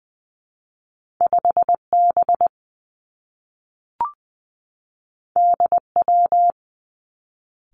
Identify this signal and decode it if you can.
Morse code, heard as a keyed tone, about 20 wpm, 700 Hz. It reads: 5 B D W
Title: Two Challenging Letter-Number Combinations 20wpm